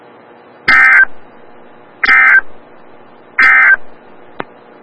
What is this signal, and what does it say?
Emergency Alert System (SAME) tones, heard as harsh buzzing bursts. It is heard from 0.6 to 3.8 s.
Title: Des Moines, IA NOAA Weather Radio - WXL57